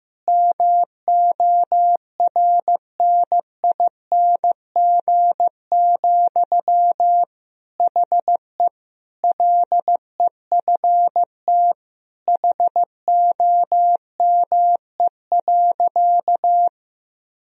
Morse code, heard as a keyed tone, about 15 wpm, 700 Hz. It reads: MORNING, HE LEFT HOME.
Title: Sentences from Top 1000 Words - Review Part 4 - 15wpm